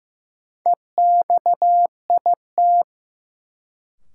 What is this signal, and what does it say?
Morse code, heard as a keyed tone, about 15 wpm, 700 Hz. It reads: EXIT
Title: Sentences from Top 1100 Words - New Word Drill Alt - 15wpm